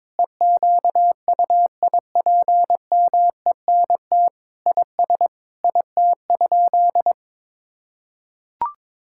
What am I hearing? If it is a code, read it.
Morse code, heard as a keyed tone, about 22 wpm, 700 Hz. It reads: EQUIPMENT IS IT?
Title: Sentences from Top 1000 Words - Review Part 5 - 22wpm